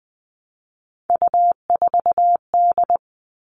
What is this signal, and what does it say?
Morse code, heard as a keyed tone, about 20 wpm, 700 Hz. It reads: U4D
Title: All Cut Numbers - 3-Digit 20wpm